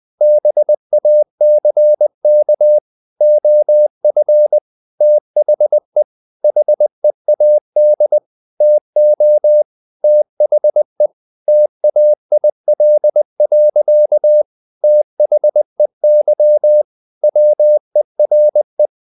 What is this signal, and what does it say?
Morse code, heard as a keyed tone, about 20 wpm, 600 Hz. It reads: BACK OF THE HEAD TO THE TAIL. THEY WERE
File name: the old man and the sea / the old man and the sea part 1 20wpm